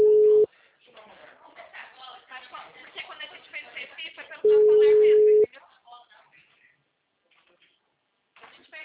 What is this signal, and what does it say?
Ringback tone (Europe, 425 Hz), with speech.